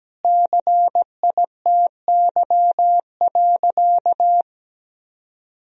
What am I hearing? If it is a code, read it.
Morse code, heard as a keyed tone, about 17 wpm, 700 Hz. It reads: CITY.
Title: Sentences from Top 1000 Words - Review Part 1 - 17wpm